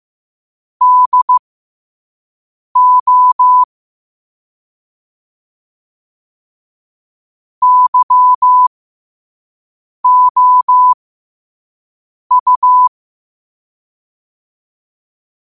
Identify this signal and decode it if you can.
Morse code, heard as a keyed tone, about 15 wpm, spaced as for 5 wpm, 1000 Hz. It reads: DO YOU
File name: qotdfiles / 5wpm